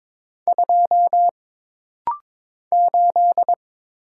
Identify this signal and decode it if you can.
Morse code, heard as a keyed tone, about 22 wpm, 700 Hz. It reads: 2 8
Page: This rapid-fire timing is geared to push your ICR (Instant Character Recognition) skills.